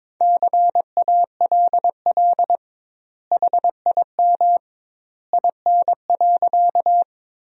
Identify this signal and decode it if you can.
Morse code, heard as a keyed tone, about 22 wpm, 700 Hz. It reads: CALL HIM IN.
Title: 2x Word-Spacing - Sentences from Top 100 Words - 22wpm